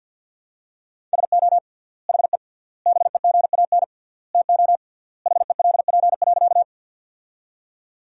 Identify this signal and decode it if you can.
Morse code, heard as a keyed tone, about 48 wpm, 700 Hz. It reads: SO HE BEGAN TO HELP.
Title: 3x Word-Spacing - Sentences from Top 300 Words - 50wpm